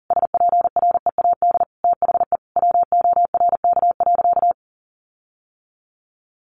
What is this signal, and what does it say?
Morse code, heard as a keyed tone, about 40 wpm, 700 Hz. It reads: SPREAD THE WORK.
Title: Sentences from Top 1400 Words 40wpm